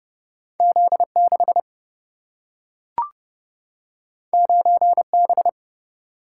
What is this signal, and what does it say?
Morse code, heard as a keyed tone, about 30 wpm, 700 Hz. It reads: Z 6 9 B
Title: Single-Letter Single-Number 30wpm